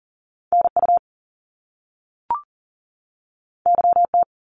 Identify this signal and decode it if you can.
Morse code, heard as a keyed tone, about 41 wpm, 700 Hz. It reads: NU YT
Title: Canadian Provinces and Territories 40wpm